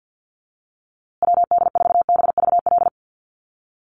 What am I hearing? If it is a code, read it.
Morse code, heard as a keyed tone, about 50 wpm, 700 Hz. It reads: WD4BVL
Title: USA Call Signs Repeated 2x 50wpm